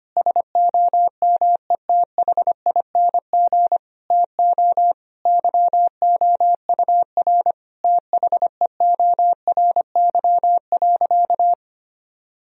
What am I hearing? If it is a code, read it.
Morse code, heard as a keyed tone, about 25 wpm, 700 Hz. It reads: SOMETHING TO YOUR THEORY.
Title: Sentences from Top 1000 Words - Review Part 3 - 25wpm